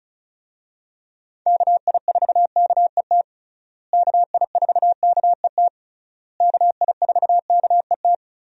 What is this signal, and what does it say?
Morse code, heard as a keyed tone, about 34 wpm, 700 Hz. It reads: KI4KET KI4KET KI4KET
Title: USA Call Signs - 6 Characters - Repeated 3x - 35wpm